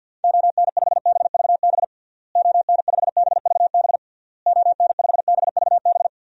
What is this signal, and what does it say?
Morse code, heard as a keyed tone, about 50 wpm, 700 Hz. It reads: ON5BVB ON5BVB ON5BVB
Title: DXCC-100 Call Signs - 6 Characters - Repeated 3x - 50wpm